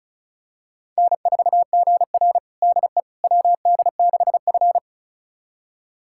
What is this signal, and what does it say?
Morse code, heard as a keyed tone, about 36 wpm, 700 Hz. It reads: N4GR DE WD6F
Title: USA Callsign DE Callsign - 4 Characters - 35wpm